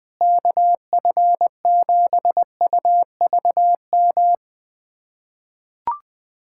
Morse code, heard as a keyed tone, about 20 wpm, 700 Hz. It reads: KF7UVM
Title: USA Callsign DE Callsign 20wpm